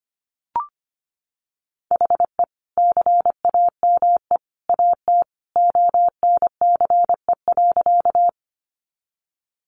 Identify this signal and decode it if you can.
Morse code, heard as a keyed tone, about 25 wpm, 700 Hz. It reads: HE CAME AT ONCE.
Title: Sentences from Top 300 Words 25wpm